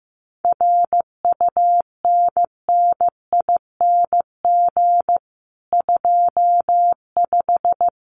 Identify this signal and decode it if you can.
Morse code, heard as a keyed tone, about 15 wpm, 700 Hz. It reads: RUNNING 25
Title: Common QSO Phrases 15wpm